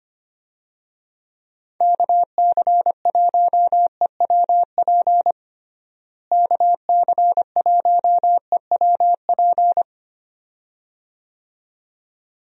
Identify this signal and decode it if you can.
Morse code, heard as a keyed tone, about 25 wpm, 700 Hz. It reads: KC1EWP KC1EWP
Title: USA Call Signs - 6 Characters - Repeated 2x - 25wpm